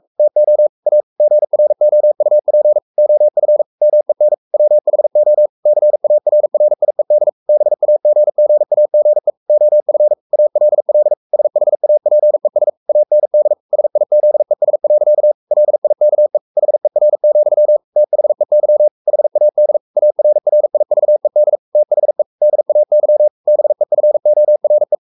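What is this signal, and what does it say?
Morse code, heard as a keyed tone, about 44 wpm, 600 Hz. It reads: TO A GROUP OF MEN WHO CARRIED BAGGAGE OF ALL SHAPES AND SIZES. LIKE HER, THEY HAD ARRIVED THE DAY BEFORE
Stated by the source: Morse code at 43 wpm sending Ken Follett's novel Never in English, from CW med Gnister.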